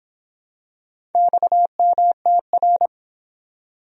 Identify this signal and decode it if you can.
Morse code, heard as a keyed tone, about 25 wpm, 700 Hz. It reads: XMTR